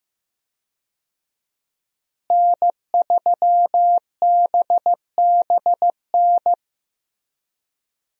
Morse code, heard as a keyed tone, about 15 wpm, 700 Hz. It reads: N3BBN
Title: USA Call Signs - 5 Characters - Repeated 2x - 15wpm